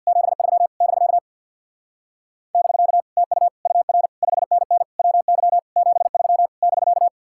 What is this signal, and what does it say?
Morse code, heard as a keyed tone, about 67 wpm, 700 Hz. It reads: 73 <BK> <BK> TU UR 5NN WY 73 <BK>